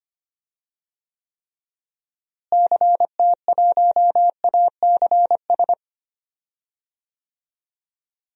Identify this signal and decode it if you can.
Morse code, heard as a keyed tone, about 25 wpm, 700 Hz. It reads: CT1ACS
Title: DXCC-100 Call Signs - 6 Characters - Repeated 2x - 25wpm